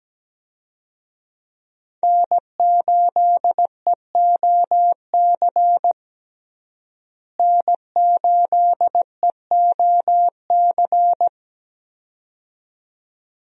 Morse code, heard as a keyed tone, about 17 wpm, 700 Hz. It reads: N8EOC N8EOC
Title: USA Call Signs - 5 Characters - Repeated 2x - 17wpm